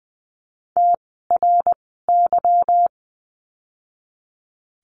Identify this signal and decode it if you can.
Morse code, heard as a keyed tone, about 20 wpm, spaced as for 14 wpm, 700 Hz. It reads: TRY
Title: 244 Lesson 39f - Letter z - Words 20wpm